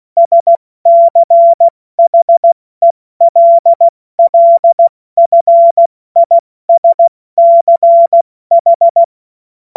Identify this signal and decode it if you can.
Morse code, heard as a keyed tone, about 16 wpm, 680 Hz. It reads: SCHELLFISCH
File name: Morsekurs / Witze / Witz 08. August 2023 70 F80